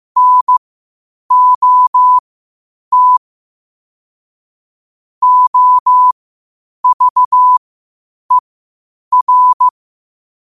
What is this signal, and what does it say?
Morse code, heard as a keyed tone, about 15 wpm, spaced as for 8 wpm, 1000 Hz. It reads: NOT OVER
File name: qotdfiles / 8wpm